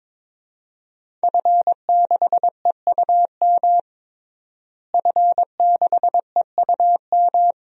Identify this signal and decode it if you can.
Morse code, heard as a keyed tone, about 21 wpm, 700 Hz. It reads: F6EUM F6EUM
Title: DXCC-100 Call Signs - 5 Characters - Repeated 2x - 22wpm